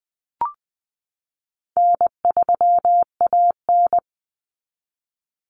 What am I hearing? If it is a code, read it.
Morse code, heard as a keyed tone, about 20 wpm, 700 Hz. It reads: N3AN
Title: Common Cut Numbers - 4-Digit 20wpm